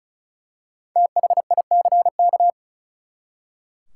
Morse code, heard as a keyed tone, about 36 wpm, 700 Hz. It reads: THICK
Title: Five Letter Words - 35wpm